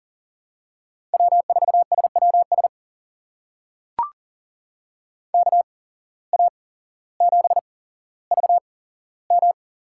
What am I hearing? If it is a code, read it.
Morse code, heard as a keyed tone, about 40 wpm, spaced as for 17 wpm, 700 Hz. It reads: W4SWS KA7VM
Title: USA Call Signs - 8x Character Spacing - 40wpm